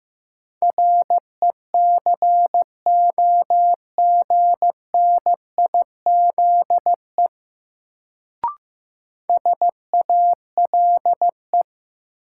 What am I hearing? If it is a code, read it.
Morse code, heard as a keyed tone, about 15 wpm, 700 Hz. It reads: RECOGNIZE SALE
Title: Sentences from Top 1400 Words - New Word Drill - 15wpm